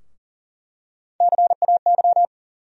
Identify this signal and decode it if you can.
Morse code, heard as a keyed tone, about 40 wpm, 700 Hz. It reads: CAY
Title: USA Call Sign Postfixes - 3 Letters 40wpm